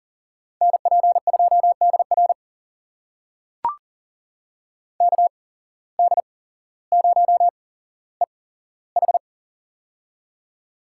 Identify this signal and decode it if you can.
Morse code, heard as a keyed tone, about 40 wpm, spaced as for 17 wpm, 700 Hz. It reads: NP2DR KD0EH
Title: USA Call Signs - 5 Characters - 8x Character Spacing - 40wpm